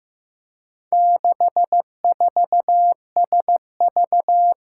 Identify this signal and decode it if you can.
Morse code, heard as a keyed tone, about 15 wpm, 700 Hz. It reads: 64SV
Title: Four Challenging Letter-Number Combinations 15wpm